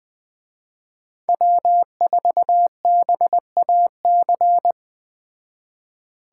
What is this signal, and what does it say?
Morse code, heard as a keyed tone, about 20 wpm, 700 Hz. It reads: W4BAC